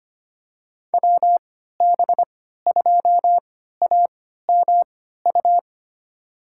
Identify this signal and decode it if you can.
Morse code, heard as a keyed tone, about 25 wpm, spaced as for 14 wpm, 700 Hz. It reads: WB2AMU
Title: USA Call Signs - 6 Characters - 3x Character Spacing - 25wpm